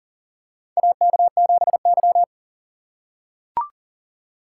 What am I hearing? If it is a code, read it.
Morse code, heard as a keyed tone, about 41 wpm, 690 Hz. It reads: AK7Y
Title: USA Call Signs - 4 Characters - 2x Character Spacing - 40wpm